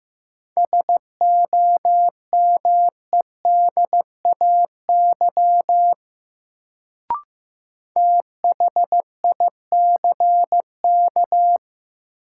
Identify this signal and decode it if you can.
Morse code, heard as a keyed tone, about 15 wpm, 700 Hz. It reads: SOMEDAY THICK
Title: Sentences from Top 1600 Words - New Word Drill - 15wpm